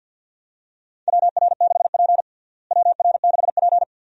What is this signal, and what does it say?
Morse code, heard as a keyed tone, about 53 wpm, 700 Hz. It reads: WR6P WR6P
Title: USA Call Signs - 4 Characters - Repeated 2x - 50wpm